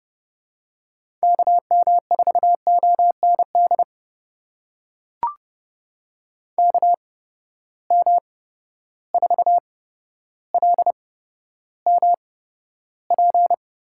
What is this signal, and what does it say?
Morse code, heard as a keyed tone, about 30 wpm, spaced as for 13 wpm, 700 Hz. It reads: KM4OND KM4LMP
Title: USA Call Signs - 6 Characters - 8x Character Spacing - 30wpm